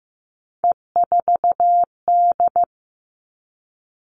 Morse code, heard as a keyed tone, about 15 wpm, 700 Hz. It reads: E4D
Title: Three Challenging Letter-Number Combinations 15wpm